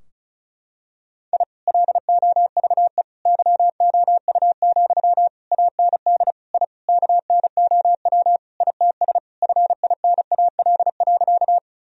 Morse code, heard as a keyed tone, about 36 wpm, 700 Hz. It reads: I LOVE YOU, AND I KNOW ITS FINAL.
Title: Sentences from Top 1000 Words - Review Part 2 - 35wpm